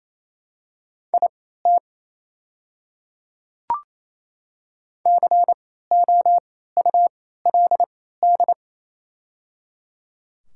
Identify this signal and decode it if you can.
Morse code, heard as a keyed tone, about 27 wpm, spaced as for 16 wpm, 700 Hz. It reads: IT COULD
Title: 3x Character Spacing - Top 200 Words - 28wpm